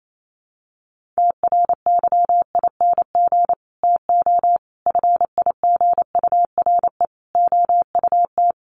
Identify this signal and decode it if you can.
Morse code, heard as a keyed tone, about 28 wpm, 700 Hz. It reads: TRYING TO FIGURE OUT
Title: Sets of 4 Words 28wpm